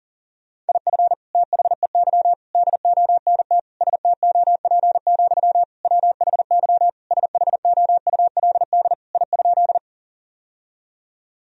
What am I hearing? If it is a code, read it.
Morse code, heard as a keyed tone, about 40 wpm, 700 Hz. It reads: IF THEY DONT STOP, WHY SHOULD I?
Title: Sentences from Top 1000 Words - Review Part 1 - 40wpm